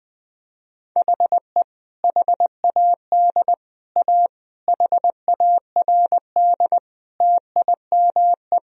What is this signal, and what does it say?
Morse code, heard as a keyed tone, about 20 wpm, 700 Hz. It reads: HE HAD A HARD TIME
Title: Sentences from Top 1000 Words - Review Part 1 - 20wpm